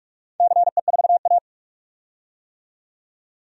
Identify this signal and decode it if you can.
Morse code, heard as a keyed tone, about 44 wpm, 700 Hz. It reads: CE4A
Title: DXCC-100 Call Signs - 4 Characters - 45wpm